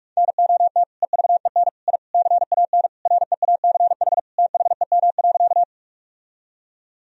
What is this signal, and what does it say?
Morse code, heard as a keyed tone, about 46 wpm, 700 Hz. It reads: NOT EVEN I CAN REACH THEM.
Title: Sentences from Top 1000 Words - Review Part 2 - 45wpm